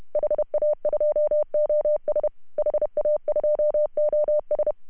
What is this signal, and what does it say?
Morse code, heard as a keyed tone, about 30 wpm, 590 Hz. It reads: HA2OS HA2OS
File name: CW / HA2OS RHOB 2017 CW